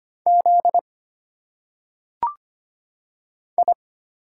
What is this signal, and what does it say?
Morse code, heard as a keyed tone, about 25 wpm, 700 Hz. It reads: Z I
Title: Single Letters - Phonetic Pronunciation 25wpm